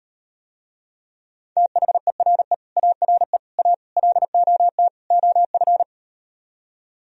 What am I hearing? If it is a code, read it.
Morse code, heard as a keyed tone, about 38 wpm, 700 Hz. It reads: THERE ARE A LOT OF